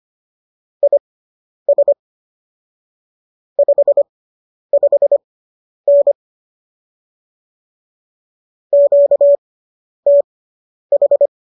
Morse code, heard as a keyed tone, about 25 wpm, spaced as for 10 wpm, 570 Hz. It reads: IS 55N QTH